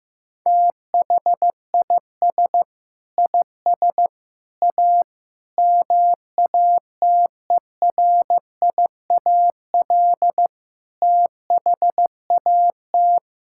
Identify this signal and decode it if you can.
Morse code, heard as a keyed tone, about 15 wpm, 700 Hz. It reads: THIS IS A MATERIAL THAT
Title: Sentences from Top 1000 Words - Review Part 5 - 15wpm